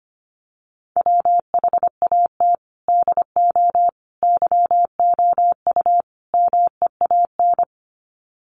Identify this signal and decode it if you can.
Morse code, heard as a keyed tone, about 25 wpm, 700 Hz. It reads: WHAT DO YOU MEAN
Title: Sets of 4 Words 25wpm